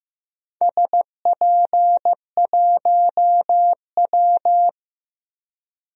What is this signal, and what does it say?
Morse code, heard as a keyed tone, about 15 wpm, 700 Hz. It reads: SP1W
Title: DXCC-100 Call Signs - 4 Characters - Repeated 3x - 15wpm